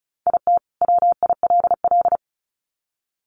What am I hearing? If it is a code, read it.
Morse code, heard as a keyed tone, about 34 wpm, 700 Hz. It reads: IT WILL
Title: Sets of 2 Words 35wpm